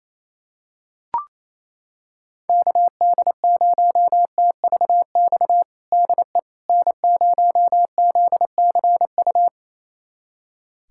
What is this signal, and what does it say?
Morse code, heard as a keyed tone, about 27 wpm, 700 Hz. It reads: KD0TVX DE N0ZCU
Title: USA Callsign DE Callsign 28wpm